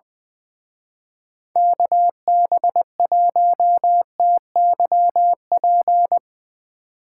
Morse code, heard as a keyed tone, about 23 wpm, 700 Hz. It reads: KB1TYP
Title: Speed Racing - USA 2x3 Call Signs 20wpm